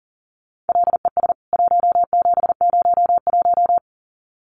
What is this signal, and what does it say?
Morse code, heard as a keyed tone, about 40 wpm, 700 Hz. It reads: LES 1701
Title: CWT Exchange 40wpm